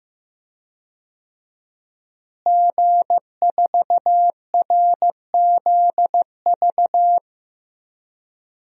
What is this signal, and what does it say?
Morse code, heard as a keyed tone, about 15 wpm, 700 Hz. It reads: G4RZV
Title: DXCC-100 Call Signs - 5 Characters - Repeated 3x - 15wpm